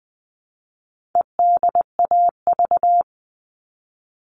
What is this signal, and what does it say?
Morse code, heard as a keyed tone, about 20 wpm, 700 Hz. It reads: EDAV
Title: All Cut Numbers - 4-Digit 20wpm